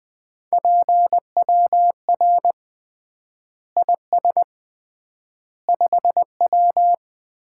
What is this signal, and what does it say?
Morse code, heard as a keyed tone, about 21 wpm, 700 Hz. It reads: PWR IS 5W
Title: Common QSO Phrases - 3x Word-Spacing - 20wpm